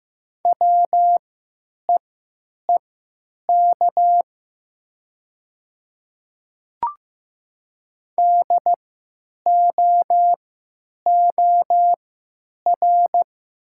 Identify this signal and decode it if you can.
Morse code, heard as a keyed tone, about 15 wpm, spaced as for 8 wpm, 700 Hz. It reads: WEEK DOOR
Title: 3x Character Spacing - Top 500 Words - 15wpm